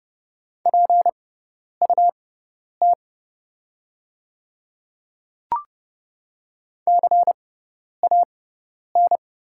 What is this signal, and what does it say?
Morse code, heard as a keyed tone, about 30 wpm, spaced as for 10 wpm, 700 Hz. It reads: PUT CAN
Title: 6x Character Spacing - Top 200 Words - 30wpm